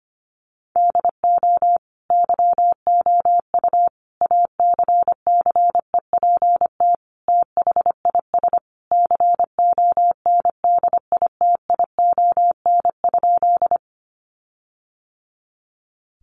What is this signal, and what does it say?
Morse code, heard as a keyed tone, about 25 wpm, 700 Hz. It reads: DO YOU ACCEPT THIS CONDITION?
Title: Sentences from Top 1300 Words 25wpm